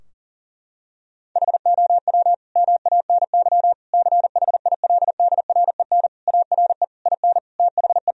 Morse code, heard as a keyed tone, about 40 wpm, 700 Hz. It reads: HOW MANY CHILDREN ARE IN THE
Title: Sentences from Top 1000 Words - Review Part 4 - 40wpm